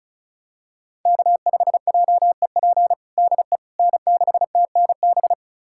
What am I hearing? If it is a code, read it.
Morse code, heard as a keyed tone, about 36 wpm, 700 Hz. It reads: K5JEP DE N6TNB
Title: USA Callsign DE Callsign - 5 Characters - 35wpm